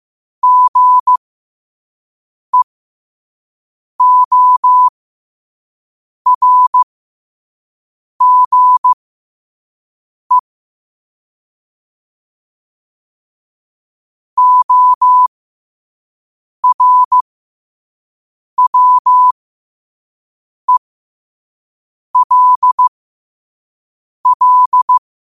Morse code, heard as a keyed tone, about 15 wpm, spaced as for 5 wpm, 1000 Hz. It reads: GEORGE ORWELL
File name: qotdfiles / 5wpm